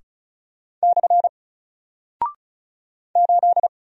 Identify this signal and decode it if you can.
Morse code, heard as a keyed tone, about 36 wpm, 700 Hz. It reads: / 8
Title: Single Character (Letter:Number:Punctuation:Prosign) Alt 35wpm